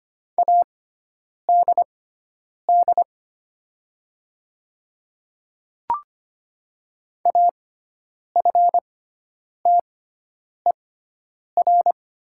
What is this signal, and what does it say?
Morse code, heard as a keyed tone, about 25 wpm, spaced as for 9 wpm, 700 Hz. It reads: ADD AFTER
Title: 6x Character Spacing - Top 200 Words - 25wpm